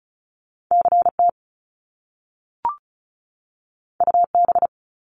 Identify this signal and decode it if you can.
Morse code, heard as a keyed tone, about 36 wpm, 700 Hz. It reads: C T U B
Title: Top Two Letter Combinations in Words 35wpm